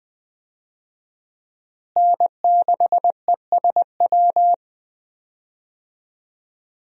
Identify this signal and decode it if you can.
Morse code, heard as a keyed tone, about 20 wpm, 700 Hz. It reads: N6ESW
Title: USA Call Signs - 5 Characters - Repeated 2x - 20wpm